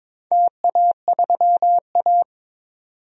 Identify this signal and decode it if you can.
Morse code, heard as a keyed tone, about 22 wpm, 700 Hz. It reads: TA3A
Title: DXCC-100 Call Signs - 4 Characters - 22wpm